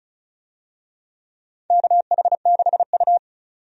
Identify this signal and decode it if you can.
Morse code, heard as a keyed tone, about 36 wpm, 690 Hz. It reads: KH6U
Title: DXCC-100 Call Signs - 4 Characters - 35wpm